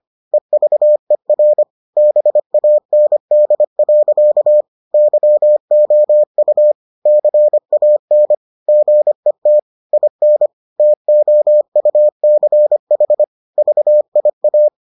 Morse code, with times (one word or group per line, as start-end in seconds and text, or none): 0.3-1.6 EVER
2.0-4.6 BAND.
4.9-6.7 YOU
7.1-8.3 CAN
8.7-9.6 GET
9.9-10.5 IN
10.8-13.2 TOUCH
13.6-14.7 VIA